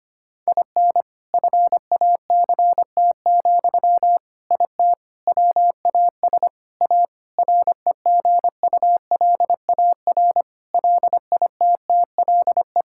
Morse code, heard as a keyed tone, about 25 wpm, 700 Hz. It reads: IN FACT, IT WAS A REGULAR LITTLE
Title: Sentences from Top 1000 Words - Review Part 6 - 25wpm